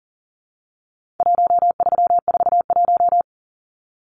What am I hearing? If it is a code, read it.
Morse code, heard as a keyed tone, about 40 wpm, 700 Hz. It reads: 1341
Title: Numbers - 4-Digit 40wpm